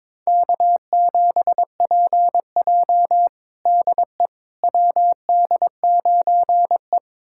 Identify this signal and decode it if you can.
Morse code, heard as a keyed tone, about 22 wpm, 700 Hz. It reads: K7PJ DE WD9E
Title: USA Callsign DE Callsign - 4 Characters - 22wpm